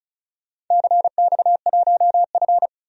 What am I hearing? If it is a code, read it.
Morse code, heard as a keyed tone, about 36 wpm, 700 Hz. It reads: CX1F
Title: DXCC-100 Call Signs - 4 Characters - 35wpm